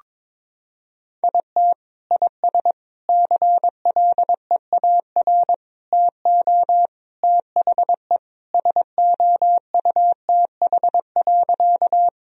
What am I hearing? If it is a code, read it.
Morse code, heard as a keyed tone, about 22 wpm, 700 Hz. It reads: IT IS CLEAR TO THE SOUTH.
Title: Sentences from Top 1000 Words - Review Part 2 - 22wpm